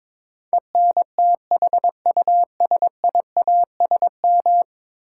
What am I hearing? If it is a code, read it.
Morse code, heard as a keyed tone, about 22 wpm, 700 Hz. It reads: ENTHUSIASM
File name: practice / 22wpm / 10 Letter Words - First 3 Letters Given - 22wpm